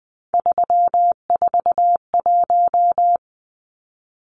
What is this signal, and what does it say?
Morse code, heard as a keyed tone, about 20 wpm, 700 Hz. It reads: 341
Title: Numbers - 3-Digit 20wpm